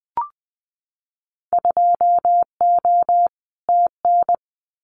Morse code, heard as a keyed tone, about 20 wpm, 700 Hz. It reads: 2O TN
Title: Winter Field Day Exchanges 20wpm